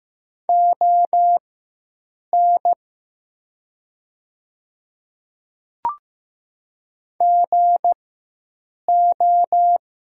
Morse code, heard as a keyed tone, about 15 wpm, spaced as for 7 wpm, 700 Hz. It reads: ON GO
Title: 4x Character Spacing - Top 100 Words - 15wpm